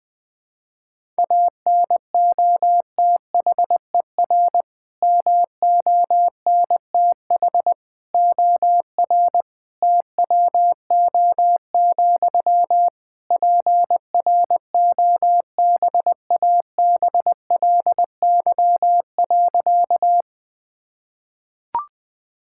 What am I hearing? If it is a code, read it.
Morse code, heard as a keyed tone, about 20 wpm, 700 Hz. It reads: ANOTHER MONTH OR TWO, PROBABLY.
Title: Sentences from Top 1000 Words - Review Part 5 - 20wpm